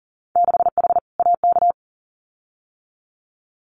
Morse code, heard as a keyed tone, about 38 wpm, 700 Hz. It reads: 6H AK
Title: Winter Field Day Exchanges Encore 40wpm